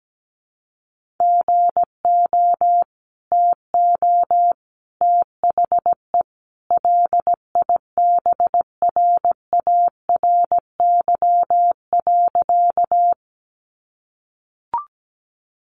Morse code, heard as a keyed tone, about 17 wpm, 700 Hz. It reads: GO TO THE LIBRARY.
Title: Sentences from Top 600 Words 17wpm